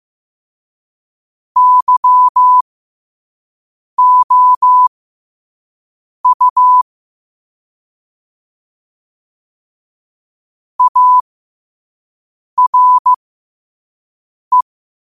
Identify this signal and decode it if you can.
Morse code, heard as a keyed tone, about 15 wpm, spaced as for 5 wpm, 1000 Hz. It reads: YOU ARE
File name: qotdfiles / 5wpm